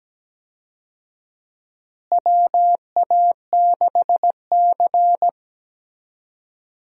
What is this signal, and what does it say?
Morse code, heard as a keyed tone, about 17 wpm, 700 Hz. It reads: WA6C
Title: USA Call Signs - 4 Characters - Repeated 3x - 17wpm